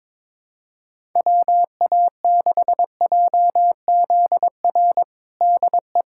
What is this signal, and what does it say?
Morse code, heard as a keyed tone, about 22 wpm, 700 Hz. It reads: WA6JZR DE
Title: USA Callsign DE Callsign - 6 Characters - 22wpm